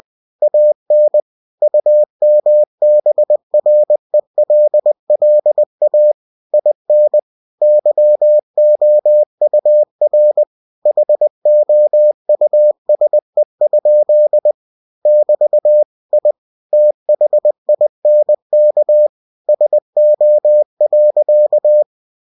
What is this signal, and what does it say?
Morse code, heard as a keyed tone, about 20 wpm, 600 Hz. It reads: AN UMBRELLA IN YOUR HOUSE? <BT> I THINK SO.
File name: cw-books / winnie-the-pooh / winnie-20wpm-page-0007